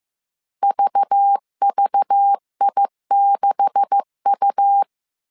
Morse code, heard as a keyed tone, about 15 wpm, 800 Hz. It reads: VVI6U
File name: CW / DET / 1 9 1500C S C 14wpm 3245